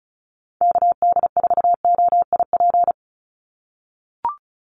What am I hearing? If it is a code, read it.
Morse code, heard as a keyed tone, about 34 wpm, 700 Hz. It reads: KD4OIP
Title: USA Call Signs 35wpm